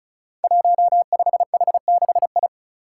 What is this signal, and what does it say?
Morse code, heard as a keyed tone, about 35 wpm, 700 Hz. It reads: 15H6I
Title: Five Challenging Letter-Number Combinations 35wpm